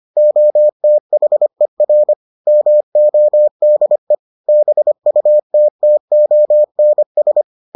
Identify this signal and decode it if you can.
Morse code, heard as a keyed tone, about 25 wpm, 600 Hz. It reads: OTHER MODE BUTTONS